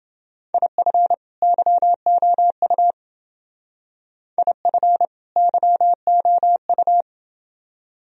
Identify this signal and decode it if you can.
Morse code, heard as a keyed tone, about 29 wpm, 700 Hz. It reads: IF YOU IF YOU